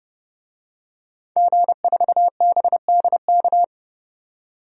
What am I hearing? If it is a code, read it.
Morse code, heard as a keyed tone, about 31 wpm, 700 Hz. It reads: G4BDK
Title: DXCC-100 Call Signs - 5 Characters - 30wpm